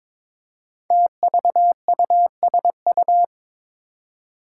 Morse code, heard as a keyed tone, about 22 wpm, 700 Hz. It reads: TVUSU
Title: Five Challenging Letter-Number Combinations 22wpm